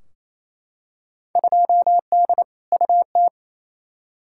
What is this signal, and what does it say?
Morse code, heard as a keyed tone, about 28 wpm, 700 Hz. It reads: 2D UT
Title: Field Day Exchanges Encore - 28wpm